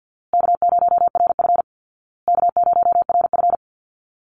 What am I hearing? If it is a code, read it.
Morse code, heard as a keyed tone, about 50 wpm, 700 Hz. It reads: K0RF K0RF
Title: Easy Call Signs Repeated 3x - 50wpm